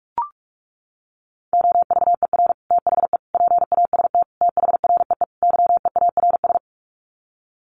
Morse code, heard as a keyed tone, about 44 wpm, 700 Hz. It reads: OVER THE PAST THREE YEARS
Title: Sets of 5 Words Encore2 45wpm